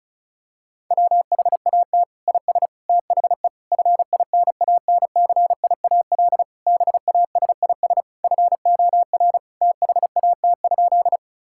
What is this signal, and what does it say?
Morse code, heard as a keyed tone, about 36 wpm, 700 Hz. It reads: WHAT IS THE FINANCIAL BASIS FOR THAT?
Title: Sentences from Top 1000 Words - Review Part 5 - 35wpm